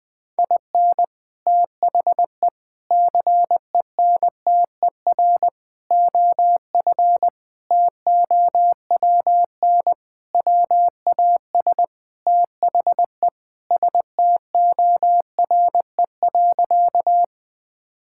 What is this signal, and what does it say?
Morse code, heard as a keyed tone, about 20 wpm, 700 Hz. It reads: IN THE CENTER OF TOWN WAS THE STORE.
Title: Sentences with 4-Word Phrases - 20wpm